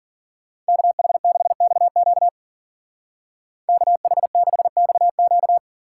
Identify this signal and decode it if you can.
Morse code, heard as a keyed tone, about 49 wpm, 700 Hz. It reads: KH6XQ KH6XQ
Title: Speed Racing - USA Call Signs 40wpm